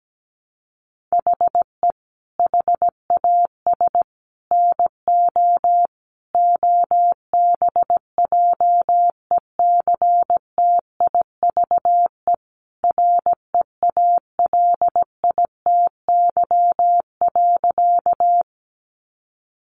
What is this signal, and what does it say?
Morse code, heard as a keyed tone, about 17 wpm, 700 Hz. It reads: HE HAS NO OBJECTIVE REALITY.
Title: Sentences from Top 1200 Words 17wpm